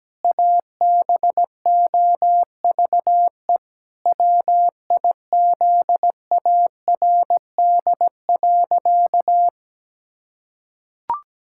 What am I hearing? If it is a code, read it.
Morse code, heard as a keyed tone, about 17 wpm, 700 Hz. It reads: ABOVE WIZARD.
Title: Pangram Challenge v2 - 17wpm